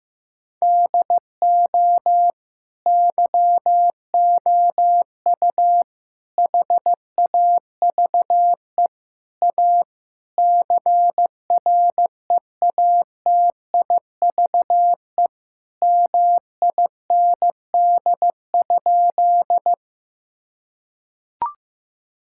Morse code, heard as a keyed tone, about 15 wpm, 700 Hz. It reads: DO YOU HAVE A CREATIVE MIND?
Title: Sentences from Top 1000 Words - Review Part 5 - 15wpm